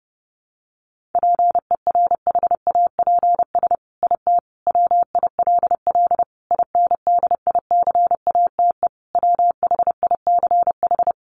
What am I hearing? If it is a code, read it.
Morse code, heard as a keyed tone, about 30 wpm, 700 Hz. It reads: PERHAPS IT WILL INDICATE WHICH
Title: Sentences from Top 1300 Words 30wpm